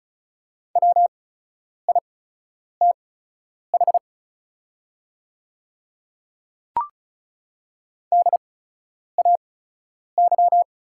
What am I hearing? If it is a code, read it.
Morse code, heard as a keyed tone, about 36 wpm, spaced as for 10 wpm, 690 Hz. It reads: WITH DAY
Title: 8x Character Spacing - Top 100 Words - 35wpm